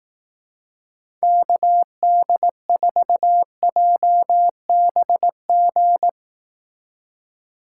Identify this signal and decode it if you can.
Morse code, heard as a keyed tone, about 18 wpm, 700 Hz. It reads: KD4JBG
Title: Speed Racing - USA 2x3 Call Signs 15wpm